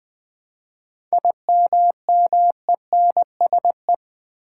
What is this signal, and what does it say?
Morse code, heard as a keyed tone, about 20 wpm, 700 Hz. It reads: IMMENSE
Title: 7 Letter Words - First 3 Letters Given - 20wpm